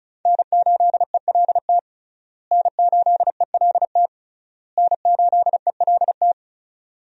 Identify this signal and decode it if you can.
Morse code, heard as a keyed tone, about 34 wpm, 700 Hz. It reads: N8ELT N8ELT N8ELT
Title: USA Call Signs - 5 Characters - Repeated 3x - 35wpm